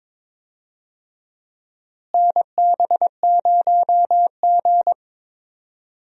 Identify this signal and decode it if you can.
Morse code, heard as a keyed tone, about 21 wpm, 700 Hz. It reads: NB0G
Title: USA Call Signs - 4 Characters - Repeated 3x - 22wpm